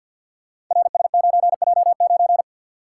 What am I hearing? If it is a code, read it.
Morse code, heard as a keyed tone, about 52 wpm, 700 Hz. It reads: RS9J9
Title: Code Groups - 5 Letter:Numbers - 50wpm